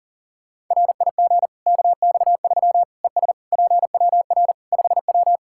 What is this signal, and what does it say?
Morse code, heard as a keyed tone, about 40 wpm, 700 Hz. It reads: RIG KX3 ES PWR 5W
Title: K1USN SST QSO Practice 40wpm